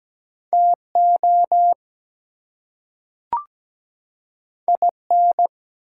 Morse code, heard as a keyed tone, about 17 wpm, 700 Hz. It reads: TO IN
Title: Two Letter Words - 17wpm